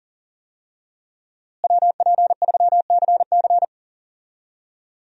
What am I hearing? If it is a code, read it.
Morse code, heard as a keyed tone, about 37 wpm, 700 Hz. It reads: WP3CC
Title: USA Call Signs - 5 Characters - Repeated 2x - 40wpm